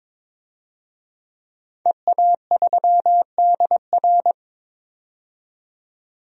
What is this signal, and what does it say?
Morse code, heard as a keyed tone, about 22 wpm, 700 Hz. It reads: EA3DR
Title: DXCC-100 Call Signs - 5 Characters - 22wpm